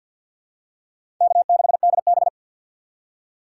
Morse code, heard as a keyed tone, about 48 wpm, 700 Hz. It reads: K6DB
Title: USA Call Signs - 4 Characters - 2x Character Spacing - 50wpm